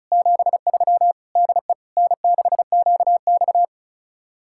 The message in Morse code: 73 DE N6QX

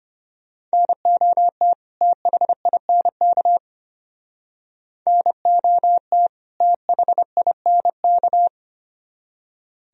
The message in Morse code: NOT THINK NOT THINK

